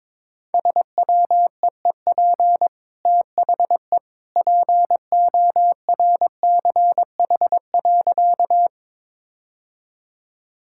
Morse code, heard as a keyed tone, about 22 wpm, 690 Hz. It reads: SWEEP THE PORCH.
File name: practice / 22wpm / Easy Sentences with Hints - 22wpm